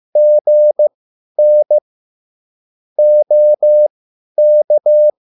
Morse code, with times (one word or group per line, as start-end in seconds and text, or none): 0.2-1.8 GN
3.0-5.1 OK